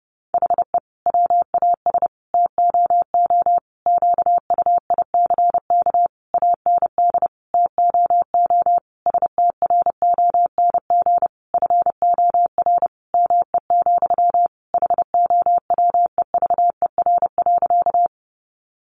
Morse code, heard as a keyed tone, about 30 wpm, 700 Hz. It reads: HE WAS TOO QUICK AND TOO STRONG FOR ME, HOWEVER.
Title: Sentences from Top 600 Words 30wpm